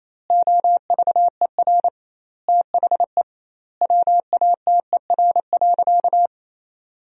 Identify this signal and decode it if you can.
Morse code, heard as a keyed tone, about 28 wpm, 700 Hz. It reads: OVER THE WATER.
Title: 2x Word-Spacing - Sentences from Top 300 Words - 28wpm